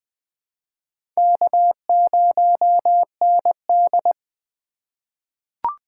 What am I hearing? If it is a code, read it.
Morse code, heard as a keyed tone, about 20 wpm, 700 Hz. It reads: K0ND
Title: USA Call Signs - 4x Character Spacing - 20wpm